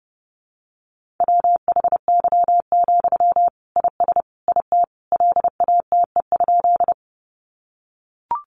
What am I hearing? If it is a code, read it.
Morse code, heard as a keyed tone, about 30 wpm, 700 Hz. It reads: WHY, IS IT LATE?
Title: Sentences from Top 300 Words 30wpm